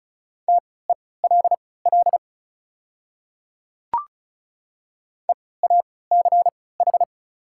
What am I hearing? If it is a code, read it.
Morse code, heard as a keyed tone, about 34 wpm, spaced as for 20 wpm, 700 Hz. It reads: TELL EACH
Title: 3x Character Spacing - Top 200 Words - 35wpm